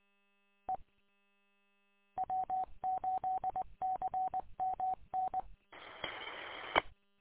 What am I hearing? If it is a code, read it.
Morse code, heard as a keyed tone, about 20 wpm, 750 Hz. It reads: E W8CMN E
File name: w8cmn- -combined